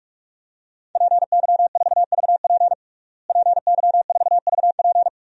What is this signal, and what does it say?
Morse code, heard as a keyed tone, about 41 wpm, 700 Hz. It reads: PY4VP PY4VP
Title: DXCC-100 Call Signs - 5 Characters - Repeated 3x - 45wpm